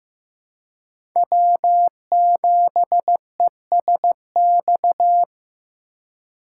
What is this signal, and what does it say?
Morse code, heard as a keyed tone, about 15 wpm, 700 Hz. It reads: W7ESX
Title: USA Call Signs - 5 Characters - 2x Character Spacing - 15wpm